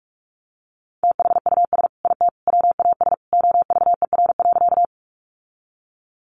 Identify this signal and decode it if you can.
Morse code, heard as a keyed tone, about 44 wpm, 700 Hz. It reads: THUS IT WAS OVER.
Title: Sentences from Top 800 Words 45wpm